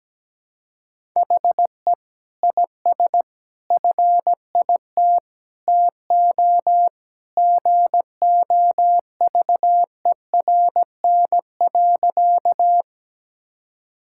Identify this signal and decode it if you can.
Morse code, heard as a keyed tone, about 17 wpm, 700 Hz. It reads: HE IS FIT TO GOVERN.
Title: Sentences from Top 1000 Words - Review Part 5 - 17wpm